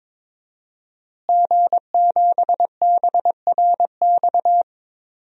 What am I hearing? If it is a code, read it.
Morse code, heard as a keyed tone, about 23 wpm, 700 Hz. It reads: G7BRX
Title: DXCC-100 Call Signs - 5 Characters - Repeated 2x - 22wpm